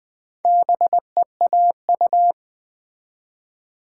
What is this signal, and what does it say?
Morse code, heard as a keyed tone, about 20 wpm, 700 Hz. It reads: BEAU
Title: Build A Word - 2 Letters At A Time - 6-Letter Words - 20wpm